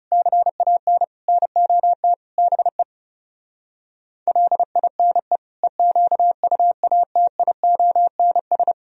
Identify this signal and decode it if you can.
Morse code, heard as a keyed tone, about 40 wpm, 700 Hz. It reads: CAN NOT BE LINE EQUATIONS